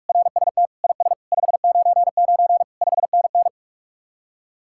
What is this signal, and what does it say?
Morse code, heard as a keyed tone, about 44 wpm, 700 Hz. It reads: RST IS 599 5NN